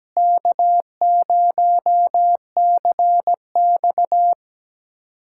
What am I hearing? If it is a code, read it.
Morse code, heard as a keyed tone, about 17 wpm, 700 Hz. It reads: K0CX
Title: USA Callsign DE Callsign - 4 Characters - 17wpm